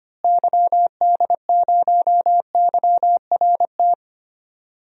Morse code, heard as a keyed tone, about 25 wpm, 700 Hz. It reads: YD0YRT
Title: DXCC-100 Call Signs - 6 Characters - Repeated 2x - 25wpm